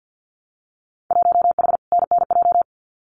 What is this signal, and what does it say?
Morse code, heard as a keyed tone, about 50 wpm, 700 Hz. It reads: 1H NNJ
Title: Winter Field Day Exchanges 50wpm